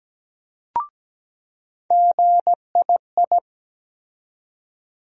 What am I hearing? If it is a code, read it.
Morse code, heard as a keyed tone, about 17 wpm, 700 Hz. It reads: GII
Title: USA Call Sign Postfixes - 3 Letters 17wpm